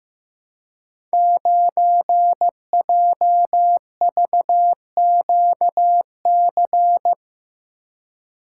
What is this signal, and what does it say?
Morse code, heard as a keyed tone, about 15 wpm, 700 Hz. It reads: 9JVQC
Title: Code Groups - 5 Letter:Numbers - 15wpm